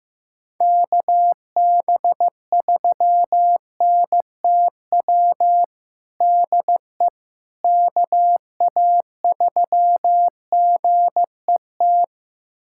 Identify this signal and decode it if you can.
Morse code, heard as a keyed tone, about 15 wpm, 700 Hz. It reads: KB3NTW DE KA3GET
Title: USA Callsign DE Callsign - 6 Characters - 15wpm